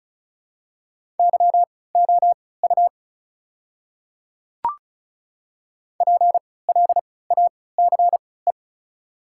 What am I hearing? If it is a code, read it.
Morse code, heard as a keyed tone, about 36 wpm, spaced as for 20 wpm, 700 Hz. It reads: YOU PLACE
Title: 3x Character Spacing - Top 500 Words - 35wpm